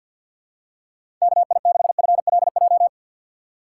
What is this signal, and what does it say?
Morse code, heard as a keyed tone, about 52 wpm, 690 Hz. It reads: KI6FLJ
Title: USA Call Signs - 6 Characters - 3x Character Spacing - 50wpm